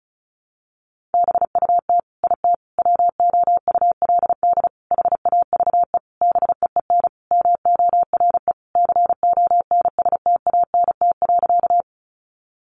Morse code, heard as a keyed tone, about 34 wpm, 700 Hz. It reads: BUT IT WOULD HAVE BEEN MORE CONSTANT.
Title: Sentences from Top 1500 Words 35wpm